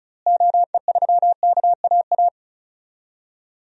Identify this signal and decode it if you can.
Morse code, heard as a keyed tone, about 34 wpm, 700 Hz. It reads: OE3KAA
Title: DXCC-100 Call Signs - 6 Characters - Repeated 2x - 35wpm